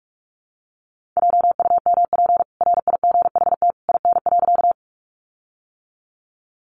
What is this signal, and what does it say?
Morse code, heard as a keyed tone, about 44 wpm, 700 Hz. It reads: JUMP RIGHT IN.
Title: Sentences from Top 1300 Words 45wpm